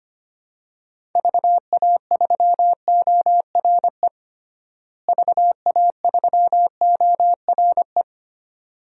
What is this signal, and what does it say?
Morse code, heard as a keyed tone, about 25 wpm, 700 Hz. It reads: VA3ORE VA3ORE
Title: DXCC-100 Call Signs - 6 Characters - Repeated 2x - 25wpm